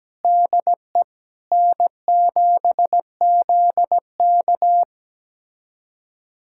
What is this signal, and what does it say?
Morse code, heard as a keyed tone, about 17 wpm, 700 Hz. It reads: DE N7ZK
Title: USA Callsign DE Callsign - 4 Characters - 17wpm